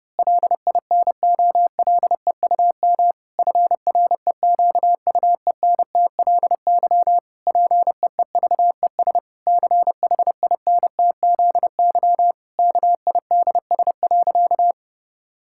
Morse code, heard as a keyed tone, about 30 wpm, 700 Hz. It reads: LINOLEUM FREQUENTLY PEEVES CHINTZY KIDS.
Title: Pangram Challenge v2 - 30wpm